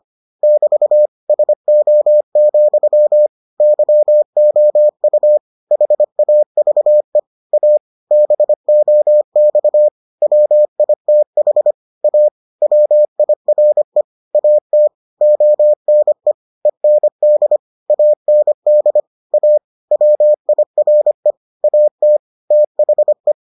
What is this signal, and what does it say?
Morse code, heard as a keyed tone, about 25 wpm, 600 Hz. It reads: <BT> SO, YOU HAVE A BOX WITH A WIRE AT ONE END AND A WIRE AT THE